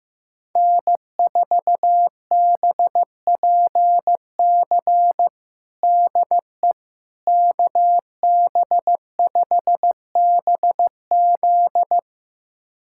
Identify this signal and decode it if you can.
Morse code, heard as a keyed tone, about 16 wpm, 700 Hz. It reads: N4BPC DE KB5BZ
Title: USA Callsign DE Callsign - 5 Characters - 15wpm